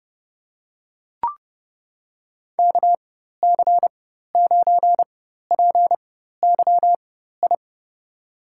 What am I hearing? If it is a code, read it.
Morse code, heard as a keyed tone, about 30 wpm, spaced as for 14 wpm, 700 Hz. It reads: KC9PYI
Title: USA Call Signs - 4x Character Spacing - 30wpm